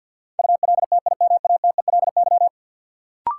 Morse code, heard as a keyed tone, about 47 wpm, 700 Hz. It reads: ULTIMATELY
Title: Sentences from Top 1200 Words - New Word Drill Alt - 50wpm